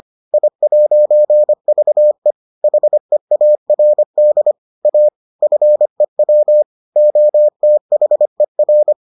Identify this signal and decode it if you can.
Morse code, heard as a keyed tone, about 25 wpm, 600 Hz. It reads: I'VE HEARD A FEW OTHER